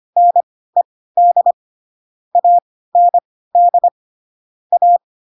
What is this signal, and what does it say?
Morse code, heard as a keyed tone, about 25 wpm, spaced as for 16 wpm, 700 Hz. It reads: NED AND A
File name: Pinocchio / Pinocchio ch2 10 25 ik1hge